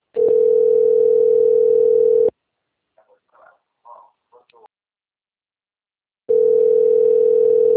Ringback tone (North America), 440 + 480 Hz.
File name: rg-1600-unknown- - - .22586